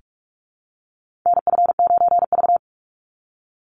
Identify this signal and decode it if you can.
Morse code, heard as a keyed tone, about 44 wpm, 700 Hz. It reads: NF9V
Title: Call Signs Repeated 2x - 45wpm